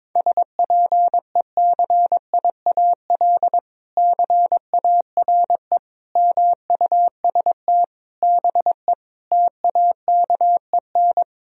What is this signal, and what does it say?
Morse code, heard as a keyed tone, about 22 wpm, 700 Hz. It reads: SPECIAL CARE MUST BE TAKEN
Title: Sentences from Top 1000 Words - Review Part 4 - 22wpm